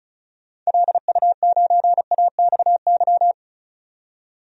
Morse code, heard as a keyed tone, about 34 wpm, 700 Hz. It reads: LU9AXY
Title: DXCC-100 Call Signs - 6 Characters - 35wpm